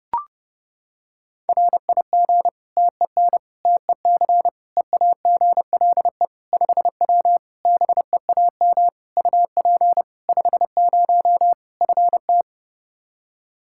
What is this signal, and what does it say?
Morse code, heard as a keyed tone, about 32 wpm, 700 Hz. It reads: RIG TEN TEC EAGLE 5W BEAM UP 50 FT
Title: Practice QSO - Exchange 2 - Info - 30wpm